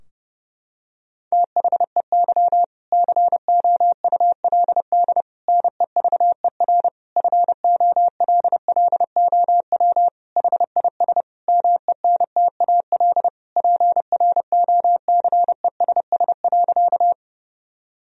Morse code, heard as a keyed tone, about 30 wpm, 700 Hz. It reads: THEY COULD NEVER FOLLOW HIS MENTAL PROCESS.
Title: Sentences from Top 1000 Words - Review Part 4 - 30wpm